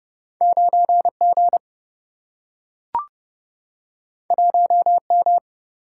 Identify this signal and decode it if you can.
Morse code, heard as a keyed tone, about 30 wpm, 700 Hz. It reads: 9G 1M
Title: Single-Letter Single-Number 30wpm